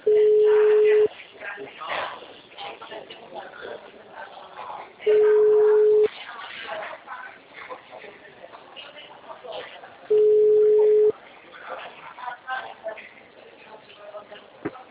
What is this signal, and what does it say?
Ringback tone (Europe, 425 Hz), with speech.
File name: out- -2015- - - .1752